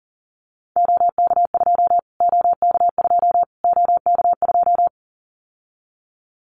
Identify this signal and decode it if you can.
Morse code, heard as a keyed tone, about 37 wpm, 700 Hz. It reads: OK2 OK2 OK2
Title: Callsign Prefixes for Top 25 DXCC - Repeated 3x - 40wpm